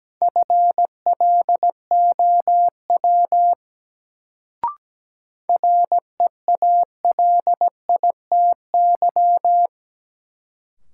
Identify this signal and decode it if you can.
Morse code, heard as a keyed tone, about 17 wpm, 700 Hz. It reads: FLOW REALITY
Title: Sentences from Top 1200 Words - New Word Drill - 17wpm